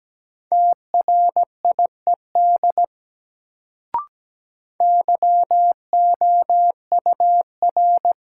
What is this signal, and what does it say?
Morse code, heard as a keyed tone, about 17 wpm, 700 Hz. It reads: TRIED YOUR
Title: Sets of 2 Words - New Word Drill - 17wpm